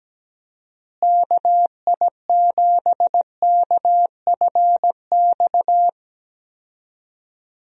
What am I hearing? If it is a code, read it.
Morse code, heard as a keyed tone, about 17 wpm, 700 Hz. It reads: KI7KFX
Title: USA Call Signs - 6 Characters - Repeated 3x - 17wpm